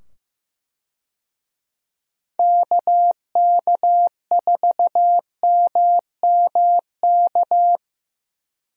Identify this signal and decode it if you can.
Morse code, heard as a keyed tone, about 15 wpm, 700 Hz. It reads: KK4MMK
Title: USA Call Signs - 6 Characters - 15wpm